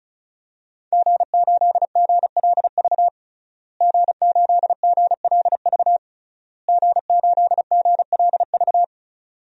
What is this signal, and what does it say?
Morse code, heard as a keyed tone, about 34 wpm, 700 Hz. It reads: G8GLV G8GLV G8GLV
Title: DXCC-100 Call Signs - 5 Characters - Repeated 3x - 35wpm